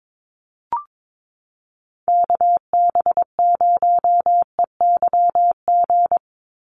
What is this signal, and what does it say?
Morse code, heard as a keyed tone, about 22 wpm, 700 Hz. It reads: KB0EYG
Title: USA Call Signs 22wpm